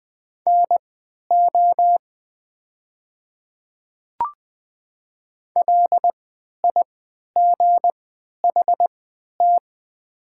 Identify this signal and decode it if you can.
Morse code, heard as a keyed tone, about 20 wpm, spaced as for 11 wpm, 700 Hz. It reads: NO LIGHT
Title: 3x Character Spacing - Top 200 Words - 20wpm